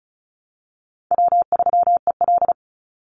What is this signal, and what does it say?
Morse code, heard as a keyed tone, about 34 wpm, 700 Hz. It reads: W3EL
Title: Call Signs - 35wpm